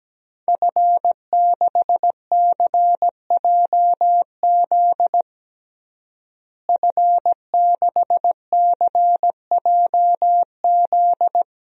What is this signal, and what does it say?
Morse code, heard as a keyed tone, about 17 wpm, 700 Hz. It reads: F6CJZ F6CJZ
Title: DXCC-100 Call Signs - 5 Characters - Repeated 2x - 17wpm